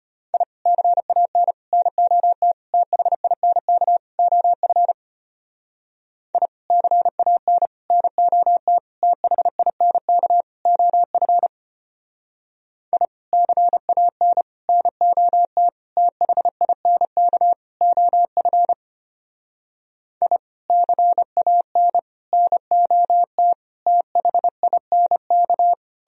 Morse code, heard as a keyed tone, about 31 wpm, 700 Hz. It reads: I CAN NOT THINK OF I CAN NOT THINK OF I CAN NOT THINK OF I CAN NOT THINK
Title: Speed-Racing - Sets of 5 Words 25wpm